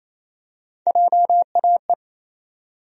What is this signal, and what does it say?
Morse code, heard as a keyed tone, about 28 wpm, 700 Hz. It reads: JAE
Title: USA Call Sign Postfixes - 3 Letters 28wpm